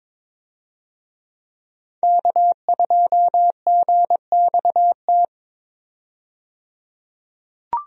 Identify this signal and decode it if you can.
Morse code, heard as a keyed tone, about 22 wpm, 700 Hz. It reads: K2GXT
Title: USA Call Signs - 5 Characters - 22wpm